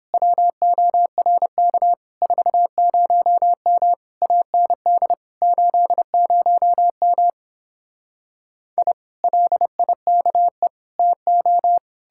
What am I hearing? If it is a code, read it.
Morse code, heard as a keyed tone, about 31 wpm, 700 Hz. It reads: WORK 40M AND 80M I LIKE TO